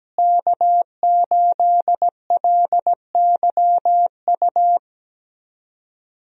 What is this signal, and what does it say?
Morse code, heard as a keyed tone, about 17 wpm, 700 Hz. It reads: K8LYU
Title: USA Call Signs - 5 Characters - Repeated 3x - 17wpm